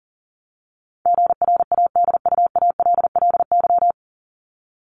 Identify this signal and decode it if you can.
Morse code, heard as a keyed tone, about 40 wpm, 700 Hz. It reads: GRADUALLY
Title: LY Suffix - Top 100 Words with Base Words 40wpm